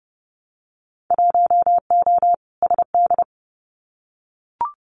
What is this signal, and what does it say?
Morse code, heard as a keyed tone, about 30 wpm, 700 Hz. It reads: 1O SD
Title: Winter Field Day Exchanges Encore 30wpm